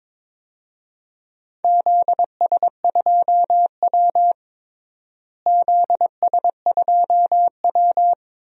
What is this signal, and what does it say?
Morse code, heard as a keyed tone, about 21 wpm, 700 Hz. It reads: ZS2W ZS2W
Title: DXCC-100 Call Signs - 4 Characters - Repeated 2x - 22wpm